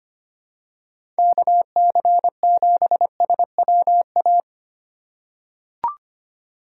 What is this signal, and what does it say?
Morse code, heard as a keyed tone, about 25 wpm, 700 Hz. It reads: KC7SWA
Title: USA Call Signs - 6 Characters - 4x Character Spacing - 25wpm